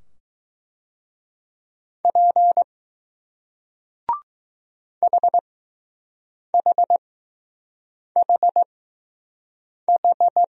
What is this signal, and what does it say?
Morse code, heard as a keyed tone, about 20 wpm, spaced as for 6 wpm, 700 Hz. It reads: P HHHH